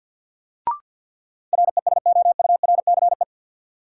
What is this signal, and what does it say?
Morse code, heard as a keyed tone, about 51 wpm, 700 Hz. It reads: RESOURCE
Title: Sentences from Top 1300 Words - New Word Drill Alt - 50wpm